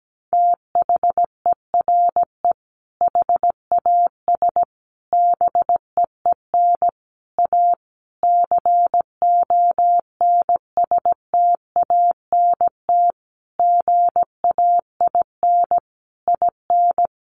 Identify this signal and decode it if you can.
Morse code, heard as a keyed tone, about 17 wpm, 700 Hz. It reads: THERE HAS BEEN A CONSTANT GAIN IN
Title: Sentences from Top 1500 Words 17wpm